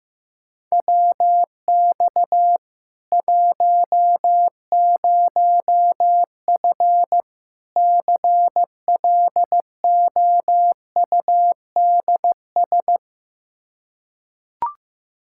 Morse code, heard as a keyed tone, about 15 wpm, 700 Hz. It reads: WX 10F CLOUDS
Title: Practice QSO - Exchange 2 - Info - 15wpm